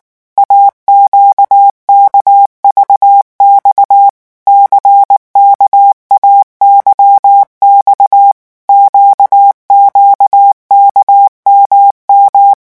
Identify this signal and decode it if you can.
Morse code, heard as a keyed tone, about 19 wpm, 800 Hz. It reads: AQKVX CKAYX QQKMM
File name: morse mp3 / revision2-vitesse 18 mots